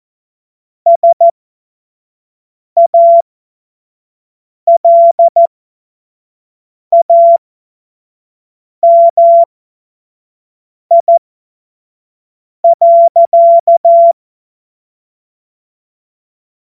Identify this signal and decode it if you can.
Morse code, heard as a keyed tone, about 14 wpm, spaced as for 5 wpm, 680 Hz. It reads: SALAMI.
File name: Morsekurs / Witze / Witz 12. Oktober 2021, 25 F70